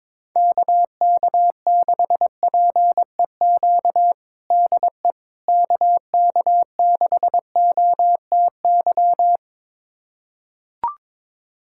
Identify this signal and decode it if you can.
Morse code, heard as a keyed tone, about 21 wpm, 700 Hz. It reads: KK6PEQ DE KK6OTY
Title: USA Callsign DE Callsign - 6 Characters - 22wpm